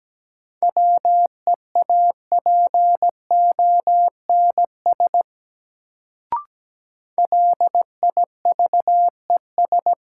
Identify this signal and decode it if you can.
Morse code, heard as a keyed tone, about 17 wpm, 700 Hz. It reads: WEAPONS LIVES
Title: Sets of 5 Words Encore - New Word Drill - 17wpm